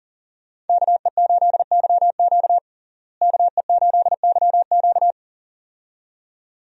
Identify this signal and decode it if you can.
Morse code, heard as a keyed tone, about 40 wpm, 700 Hz. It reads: KE8YQ KE8YQ
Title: USA Call Signs - 5 Characters - Repeated 2x - 40wpm